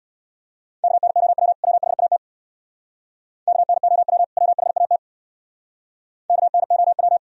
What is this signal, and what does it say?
Morse code, heard as a keyed tone, about 61 wpm, 700 Hz. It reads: KNOW WHAT KNOW WHAT KNOW
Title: Speed-Racing - Sets of 2 Words 50wpm